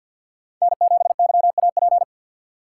Morse code, heard as a keyed tone, about 53 wpm, 700 Hz. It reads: N7YAP
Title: USA Call Signs - 5 Characters - 6x Character Spacing - 50wpm